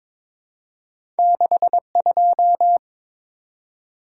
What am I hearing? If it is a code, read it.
Morse code, heard as a keyed tone, about 23 wpm, 690 Hz. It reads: 62
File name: practice / 22wpm / Two Letter-Number Combinations - 22wpm